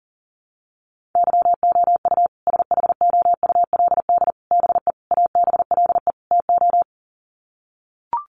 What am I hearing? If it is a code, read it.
Morse code, heard as a keyed tone, about 40 wpm, 700 Hz. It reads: YOU SHOULD BE ABLE TO
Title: Sets of 5 Words Encore 40wpm